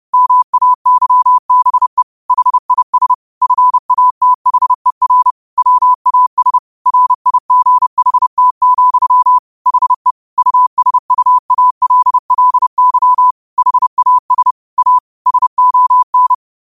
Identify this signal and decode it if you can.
Morse code, heard as a keyed tone, about 30 wpm, 1000 Hz. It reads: MAYBE HIS FATHER WAS RIGHT, HE USUALLY HAS A SON